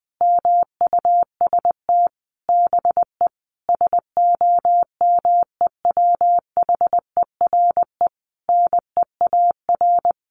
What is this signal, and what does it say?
Morse code, heard as a keyed tone, about 19 wpm, 700 Hz. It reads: MUST BE SOMEWHERE NEAR
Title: Sentences from Top 1400 Words 20wpm